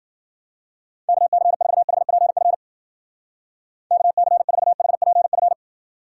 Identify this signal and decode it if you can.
Morse code, heard as a keyed tone, about 49 wpm, 700 Hz. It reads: KC4HPF KC4HPF
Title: Speed Racing - USA Call Signs 40wpm